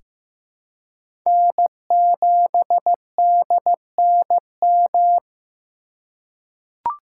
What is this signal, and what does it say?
Morse code, heard as a keyed tone, about 15 wpm, 700 Hz. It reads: N7DNM
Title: USA Call Signs - 5 Characters - 3x Character Spacing - 15wpm